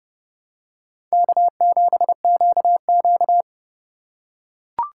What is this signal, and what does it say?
Morse code, heard as a keyed tone, about 30 wpm, 700 Hz. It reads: K7QQ
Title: USA Call Signs - 4 Characters - 2x Character Spacing - 30wpm